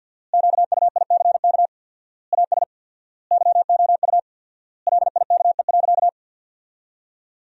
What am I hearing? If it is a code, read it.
Morse code, heard as a keyed tone, about 53 wpm, 700 Hz. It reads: QUICK AS YOU LIKE.
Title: 4x Word-Spacing - Sentences from Top 500 Words - 50wpm